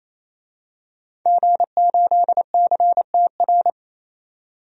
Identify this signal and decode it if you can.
Morse code, heard as a keyed tone, about 28 wpm, 700 Hz. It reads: G8CTR
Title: DXCC-100 Call Signs - 5 Characters - 28wpm